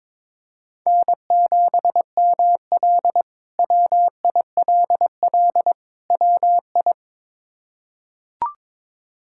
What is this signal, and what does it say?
Morse code, heard as a keyed tone, about 22 wpm, 700 Hz. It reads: N7ML WILL WI
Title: K1USN SST QSO Practice - 22wpm